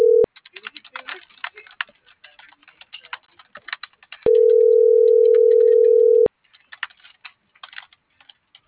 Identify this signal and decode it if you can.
Ringback tone (North America), 440 + 480 Hz.